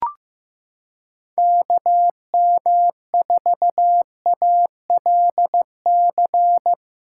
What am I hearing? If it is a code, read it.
Morse code, heard as a keyed tone, about 15 wpm, 700 Hz. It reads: KM4ALC
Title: USA Callsign DE Callsign 15wpm